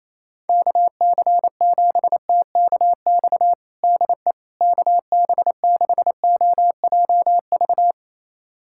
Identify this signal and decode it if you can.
Morse code, heard as a keyed tone, about 28 wpm, 700 Hz. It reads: KC7TKX DE KB6OJV
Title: USA Callsign DE Callsign - 6 Characters - 28wpm